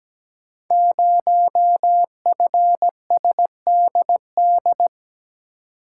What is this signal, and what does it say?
Morse code, heard as a keyed tone, about 17 wpm, 690 Hz. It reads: 0FSDD
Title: Code Groups - 5 Letter:Numbers - 17wpm